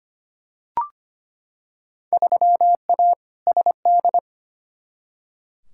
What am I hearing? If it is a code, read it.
Morse code, heard as a keyed tone, about 25 wpm, 700 Hz. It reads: 3A SD
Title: Field Day Exchanges - 25wpm